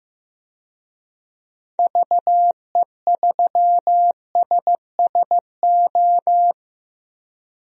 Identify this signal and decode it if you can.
Morse code, heard as a keyed tone, about 15 wpm, 700 Hz. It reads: VE3SSO
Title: DXCC-100 Call Signs - 6 Characters - Repeated 3x - 15wpm